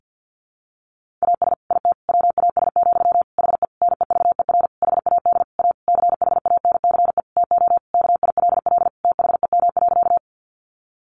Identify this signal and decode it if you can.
Morse code, heard as a keyed tone, about 50 wpm, 700 Hz. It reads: AS IT WAS, HE NEVER HAD A CHANCE TO KILL THEM.
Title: Sentences from Top 1400 Words 50wpm